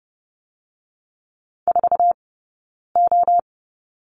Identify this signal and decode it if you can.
Morse code, heard as a keyed tone, about 30 wpm, spaced as for 9 wpm, 700 Hz. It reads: 4O
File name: Lesson 20d - Number 3 - Two Characters 30wpm